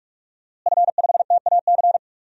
Morse code, heard as a keyed tone, about 46 wpm, 700 Hz. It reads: F5TAC
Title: DXCC-100 Call Signs - 5 Characters - Repeated 2x - 45wpm